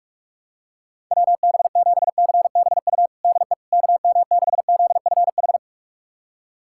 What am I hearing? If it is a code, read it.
Morse code, heard as a keyed tone, about 46 wpm, 700 Hz. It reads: WB7CBU DE KM6ZFH
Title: USA Callsign DE Callsign - 6 Characters - 45wpm